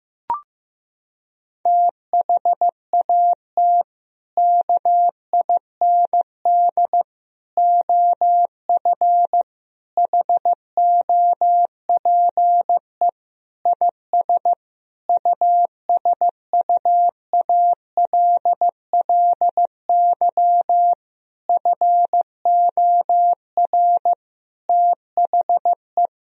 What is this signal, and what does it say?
Morse code, heard as a keyed tone, about 15 wpm, 700 Hz. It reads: THAT KIND OF HOPE IS USUALLY FOR THE
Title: Sentences from Top 1000 Words - Review Part 3 - 15wpm